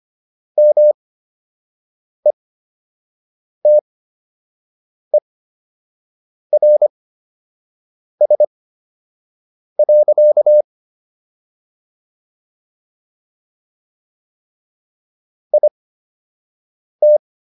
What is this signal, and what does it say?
Morse code, heard as a keyed tone, about 25 wpm, spaced as for 6 wpm, 600 Hz. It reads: METERS. IT